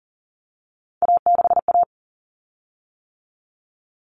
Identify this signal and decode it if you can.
Morse code, heard as a keyed tone, about 40 wpm, 700 Hz. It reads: A6A
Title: Common Cut Numbers - 3-Digit 40wpm